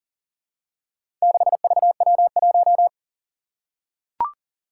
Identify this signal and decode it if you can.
Morse code, heard as a keyed tone, about 40 wpm, 700 Hz. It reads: 6VW1
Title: Four Challenging Letter-Number Combinations 40wpm